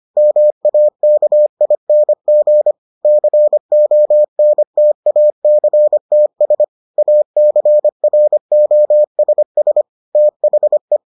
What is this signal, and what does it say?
Morse code, heard as a keyed tone, about 25 wpm, 600 Hz. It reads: MAKING CONTACTS ACROSS THE